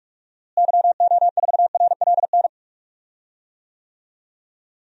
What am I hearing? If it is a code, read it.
Morse code, heard as a keyed tone, about 41 wpm, 700 Hz. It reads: YO4RLN
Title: DXCC-100 Call Signs - 6 Characters - Repeated 2x - 45wpm